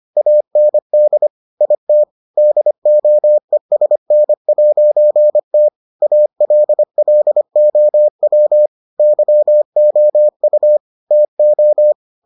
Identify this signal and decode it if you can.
Morse code, heard as a keyed tone, about 25 wpm, 600 Hz. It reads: AND IT DOESN'T ALLOW YOU TO